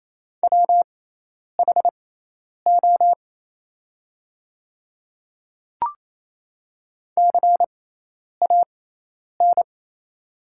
Morse code, heard as a keyed tone, about 27 wpm, spaced as for 10 wpm, 700 Hz. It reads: WHO CAN
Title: 6x Character Spacing - Top 200 Words - 28wpm